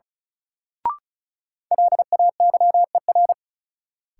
Morse code, heard as a keyed tone, about 36 wpm, 700 Hz. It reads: LAYER
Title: Sentences from Top 1600 Words - New Word Drill - 35wpm